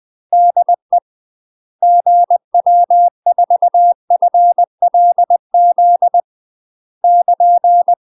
Morse code, heard as a keyed tone, about 20 wpm, 700 Hz. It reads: DE GW4FLZ <KN>
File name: UK MASTER QSOs TRAINING / 20WPM / UK MASTER QSOs 0026